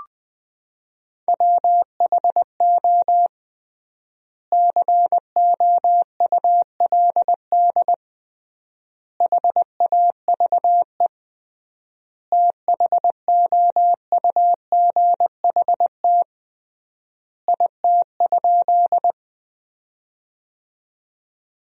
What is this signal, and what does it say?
Morse code, heard as a keyed tone, about 21 wpm, 700 Hz. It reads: WHO COULD HAVE THOUGHT IT?
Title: 3x Word-Spacing - Sentences from Top 300 Words - 20wpm